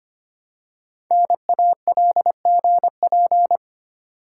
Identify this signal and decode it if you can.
Morse code, heard as a keyed tone, about 25 wpm, 700 Hz. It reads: NALGP
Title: Code Groups - 5 Letters - 25wpm